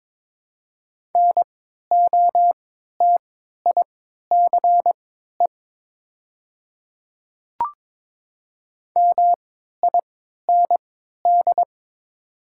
Morse code, heard as a keyed tone, about 22 wpm, spaced as for 12 wpm, 700 Hz. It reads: NOTICE MIND
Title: 3x Character Spacing - Top 500 Words - 22wpm